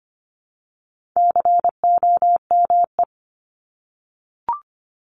Morse code, heard as a keyed tone, about 25 wpm, 700 Hz. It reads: COME
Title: Top 300 Words 25wpm